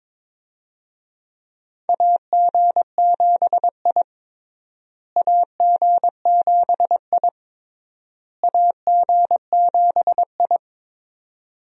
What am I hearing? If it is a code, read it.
Morse code, heard as a keyed tone, about 21 wpm, 700 Hz. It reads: AG7I AG7I AG7I
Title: USA Call Signs - 4 Characters - Repeated 3x - 22wpm